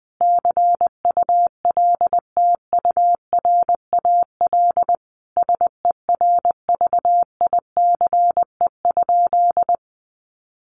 Morse code, heard as a keyed tone, about 19 wpm, 700 Hz. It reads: CULTURAL SERVICE?
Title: Sentences from Top 1500 Words 20wpm